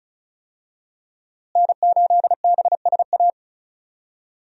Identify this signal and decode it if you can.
Morse code, heard as a keyed tone, about 36 wpm, 700 Hz. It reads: N8BSA
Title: USA Call Signs - 5 Characters - Repeated 3x - 35wpm